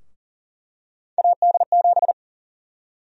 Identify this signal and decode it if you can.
Morse code, heard as a keyed tone, about 40 wpm, 700 Hz. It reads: AD7
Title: USA Call Sign Prefixes - Repeated 2x - 40wpm